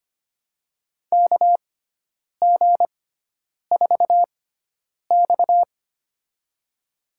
Morse code, heard as a keyed tone, about 25 wpm, spaced as for 9 wpm, 700 Hz. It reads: KG4X
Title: USA Call Signs - 6x Character Spacing - 25wpm